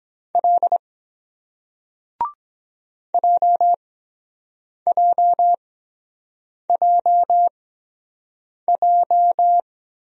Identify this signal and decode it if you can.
Morse code, heard as a keyed tone, about 23 wpm, spaced as for 7 wpm, 700 Hz. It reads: L JJJJ